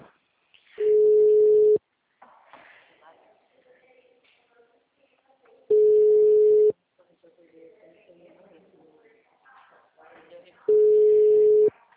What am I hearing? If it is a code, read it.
Ringback tone (Europe, 425 Hz), with speech.
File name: out- -2028- - - .42